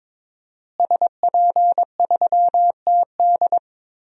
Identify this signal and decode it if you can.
Morse code, heard as a keyed tone, about 22 wpm, 700 Hz. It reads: SP3TD
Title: DXCC-100 Call Signs - 5 Characters - Repeated 3x - 22wpm